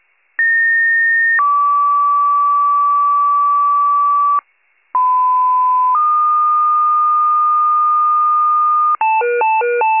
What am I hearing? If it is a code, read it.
Two-tone page, heard as a one-second tone, then a three-second tone, about 1850 then 1150 Hz, about 1010 then 1220 Hz. It is heard from 0.4 to 9.0 s.